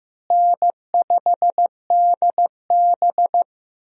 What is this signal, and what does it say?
Morse code, heard as a keyed tone, about 15 wpm, 700 Hz. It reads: N5DB
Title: USA Call Signs - 4 Characters - Repeated 3x - 15wpm